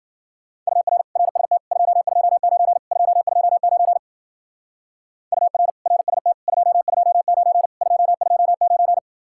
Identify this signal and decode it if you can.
Morse code, heard as a keyed tone, about 52 wpm, 700 Hz. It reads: UR RST 229 229 UR RST 229 229